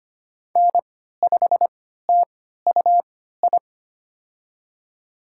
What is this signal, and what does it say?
Morse code, heard as a keyed tone, about 25 wpm, spaced as for 14 wpm, 700 Hz. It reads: N5TUI
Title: USA Call Signs - 5 Characters - 3x Character Spacing - 25wpm